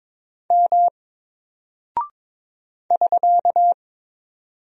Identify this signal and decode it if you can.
Morse code, heard as a keyed tone, about 22 wpm, 700 Hz. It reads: M <SK>
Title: Single Character (Letter:Number:Punctuation:Prosign) Alt 22wpm